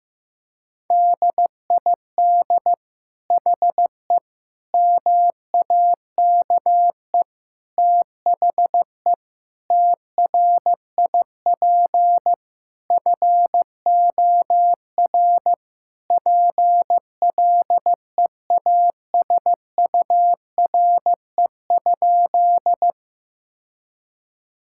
Morse code, heard as a keyed tone, about 16 wpm, 700 Hz. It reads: DID HE MAKE THE TRIP FOR PLEASURE?
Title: Sentences from Top 1600 Words 15wpm